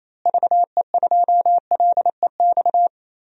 Morse code, heard as a keyed tone, about 28 wpm, 700 Hz. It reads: VE2LEX
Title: DXCC-100 Call Signs - 6 Characters - Repeated 2x - 28wpm